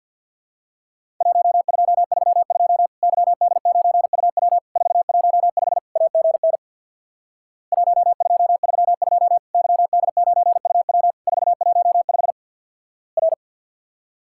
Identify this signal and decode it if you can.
Morse code, heard as a keyed tone, about 50 wpm, 700 Hz. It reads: 1232 YD9UW 415 AGN 1232 YD9UW 415 R
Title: Worked All Europe DX Contest - QTC Exchange with Repeats - 50wpm